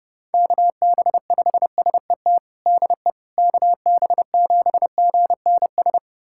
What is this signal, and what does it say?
Morse code, heard as a keyed tone, about 32 wpm, 700 Hz. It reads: KB5SET DE KB7GNS
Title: USA Callsign DE Callsign - 6 Characters - 30wpm